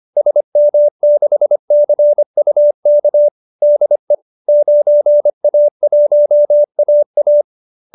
Morse code, heard as a keyed tone, about 25 wpm, 570 Hz. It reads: SM6CUK DE 9A1AA